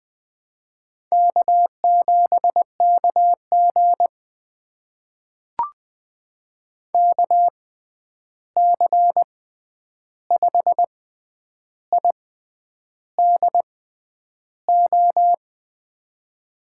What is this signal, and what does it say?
Morse code, heard as a keyed tone, about 20 wpm, spaced as for 9 wpm, 700 Hz. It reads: K7KG KC5IDO
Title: USA Call Signs - 6x Character Spacing - 20wpm